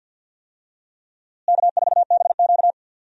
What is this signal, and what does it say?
Morse code, heard as a keyed tone, about 50 wpm, 700 Hz. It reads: K4BQ
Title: USA Call Signs - 4 Characters - Repeated 2x - 50wpm